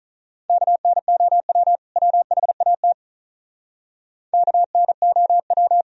Morse code, heard as a keyed tone, about 43 wpm, 700 Hz. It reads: KNOW WHAT KNOW
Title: Speed-Racing - Sets of 2 Words 35wpm